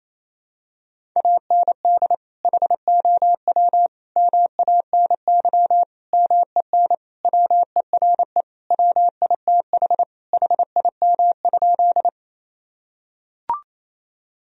Morse code, heard as a keyed tone, about 28 wpm, 700 Hz. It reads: AND HOW MANY MEN WERE WITH HIM?
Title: Sentences from Top 1000 Words - Review Part 1 - 28wpm